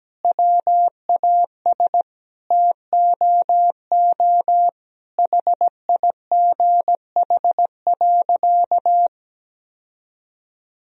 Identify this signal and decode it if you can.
Morse code, heard as a keyed tone, about 17 wpm, 700 Hz. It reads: WAS TOO HIGH.
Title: Sentences from Top 1000 Words - Review Part 4 - 17wpm